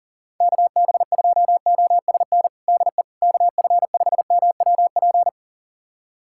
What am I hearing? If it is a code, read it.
Morse code, heard as a keyed tone, about 40 wpm, 700 Hz. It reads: KB2OSN DE KF5MWP
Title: USA Callsign DE Callsign 40wpm